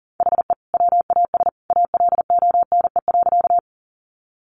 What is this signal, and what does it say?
Morse code, heard as a keyed tone, about 40 wpm, 700 Hz. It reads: HE WAS ALONE.
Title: Sentences from Top 1000 Words 40wpm